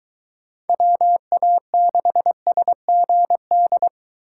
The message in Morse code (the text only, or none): WA6SGD